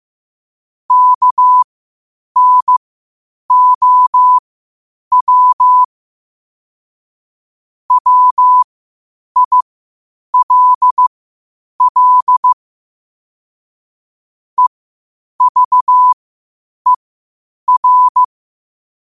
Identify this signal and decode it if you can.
Morse code, heard as a keyed tone, about 15 wpm, spaced as for 8 wpm, 1000 Hz. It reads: KNOW WILL EVER
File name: qotdfiles / 8wpm